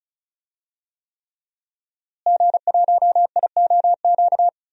Morse code, heard as a keyed tone, about 36 wpm, 700 Hz. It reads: G1IOQ
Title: DXCC-100 Call Signs - 5 Characters - Repeated 3x - 35wpm